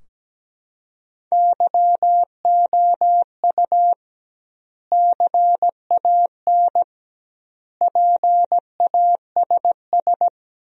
Morse code, heard as a keyed tone, about 17 wpm, 700 Hz. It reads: YOU CAN PASS
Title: 2x Word-Spacing - Sentences from Top 400 Words - 17wpm